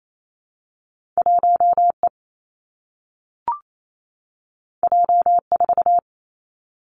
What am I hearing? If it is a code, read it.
Morse code, heard as a keyed tone, about 28 wpm, 700 Hz. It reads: 1 E J 4
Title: Two Challenging Letter-Number Combinations 28wpm